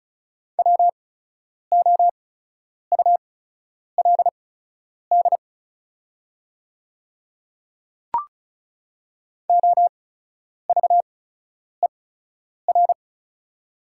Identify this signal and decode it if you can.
Morse code, heard as a keyed tone, about 36 wpm, spaced as for 10 wpm, 700 Hz. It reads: WOULD OVER
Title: 8x Character Spacing - Top 100 Words - 35wpm